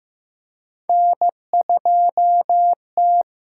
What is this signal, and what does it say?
Morse code, heard as a keyed tone, about 15 wpm, 700 Hz. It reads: N2T
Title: USA Call Signs - 6x Character Spacing - 15wpm